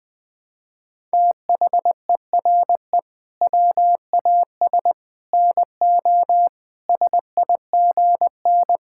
Morse code, heard as a keyed tone, about 20 wpm, 700 Hz. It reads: THERE WAS NO SIGN
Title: Sentences from Top 1000 Words - Review Part 6 - 20wpm